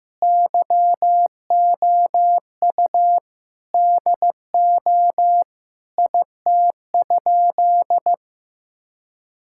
Morse code, heard as a keyed tone, about 15 wpm, 700 Hz. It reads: YOU DO IT?
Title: Sentences from Top 1000 Words - Review Part 5 - 15wpm